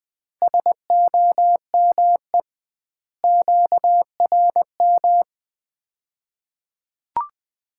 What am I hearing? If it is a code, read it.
Morse code, heard as a keyed tone, about 20 wpm, 700 Hz. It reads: SOME QRM
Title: Common QSO Phrases - 2x Word-Spacing - 20wpm